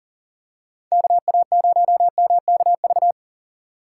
Morse code, heard as a keyed tone, about 40 wpm, 700 Hz. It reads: KA0MKV
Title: USA Call Signs - 6 Characters - 2x Character Spacing - 40wpm